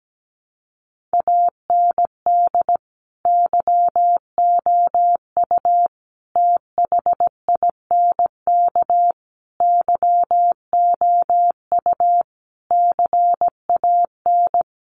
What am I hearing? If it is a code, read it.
Morse code, heard as a keyed tone, about 17 wpm, 700 Hz. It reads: AND YOU THINK YOU CAN
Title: Sentences from Top 1500 Words 17wpm